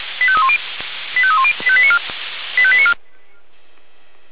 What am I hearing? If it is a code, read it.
Five-tone selective call (ZVEI), heard as quick runs of five tones, tones of about 70 ms. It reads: 85310 85310 85894